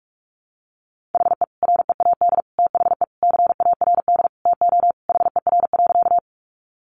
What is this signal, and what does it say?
Morse code, heard as a keyed tone, about 44 wpm, 700 Hz. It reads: HE READ THE CARD TO HER.
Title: Sentences from Top 600 Words 45wpm